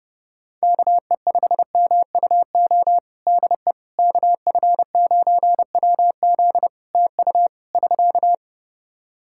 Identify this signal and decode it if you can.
Morse code, heard as a keyed tone, about 30 wpm, 700 Hz. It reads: KE5MUO DE KF9WZ TU <SK>
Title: K1USN SST QSO Practice 30wpm